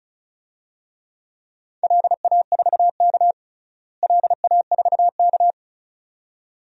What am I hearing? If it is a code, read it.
Morse code, heard as a keyed tone, about 36 wpm, 700 Hz. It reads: LA4K LA4K
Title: DXCC-100 Call Signs - 4 Characters - Repeated 2x - 35wpm